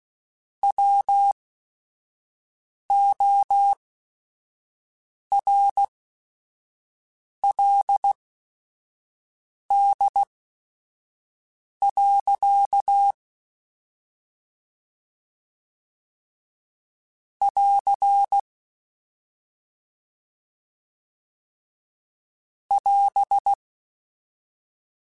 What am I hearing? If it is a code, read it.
Morse code, heard as a keyed tone, about 16 wpm, spaced as for 5 wpm, 800 Hz. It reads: WORLD. <AR> <AS>
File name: Quotes / Quotes-mp3s / ChildAsleep @5